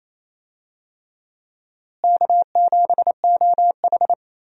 Morse code, heard as a keyed tone, about 28 wpm, 700 Hz. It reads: K7OH
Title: USA Call Signs - 4 Characters - Repeated 3x - 28wpm